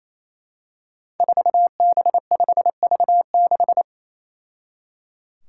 Morse code, heard as a keyed tone, about 28 wpm, 700 Hz. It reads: 4B5V6
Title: Five Challenging Letter-Number Combinations 28wpm